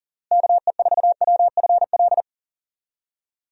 Morse code, heard as a keyed tone, about 40 wpm, 700 Hz. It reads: KE4WFL
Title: USA Call Signs - 6 Characters - Repeated 3x - 40wpm